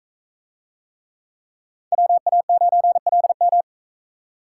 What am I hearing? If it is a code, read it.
Morse code, heard as a keyed tone, about 41 wpm, 700 Hz. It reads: WA9LM